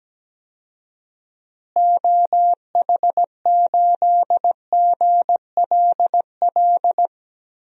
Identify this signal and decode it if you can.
Morse code, heard as a keyed tone, about 17 wpm, 700 Hz. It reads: OH8GLL
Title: DXCC-100 Call Signs - 6 Characters - Repeated 3x - 17wpm